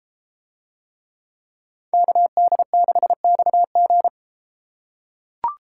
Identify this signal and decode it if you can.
Morse code, heard as a keyed tone, about 33 wpm, 700 Hz. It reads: KD6XG